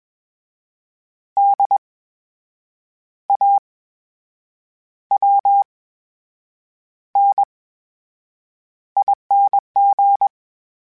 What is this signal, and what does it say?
Morse code, heard as a keyed tone, about 21 wpm, spaced as for 7 wpm, 800 Hz. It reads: DAWNING
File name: Special INGs-mp3s / Special INGs-mp3s / VerySlowToFaster Ing @20